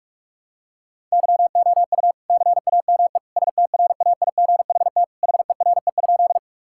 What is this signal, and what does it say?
Morse code, heard as a keyed tone, about 46 wpm, 700 Hz. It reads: YOU CAME STRAIGHT HERE?
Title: Sentences from Top 1000 Words - Review Part 6 - 45wpm